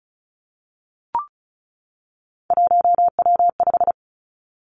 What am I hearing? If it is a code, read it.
Morse code, heard as a keyed tone, about 36 wpm, 700 Hz. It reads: 1W5
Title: Three Challenging Letter-Number Combinations 35wpm